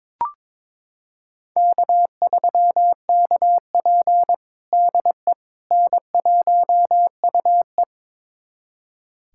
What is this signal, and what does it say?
Morse code, heard as a keyed tone, about 22 wpm, 700 Hz. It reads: K3KP DE N1UE
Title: USA Callsign DE Callsign - 4 Characters - 22wpm